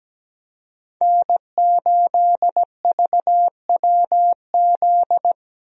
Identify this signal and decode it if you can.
Morse code, heard as a keyed tone, about 17 wpm, 700 Hz. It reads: N8VWZ
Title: USA Call Signs - 5 Characters - Repeated 3x - 17wpm